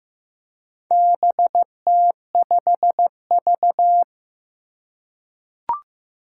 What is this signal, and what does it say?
Morse code, heard as a keyed tone, about 15 wpm, 700 Hz. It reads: BT5V
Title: Four Challenging Letter-Number Combinations 15wpm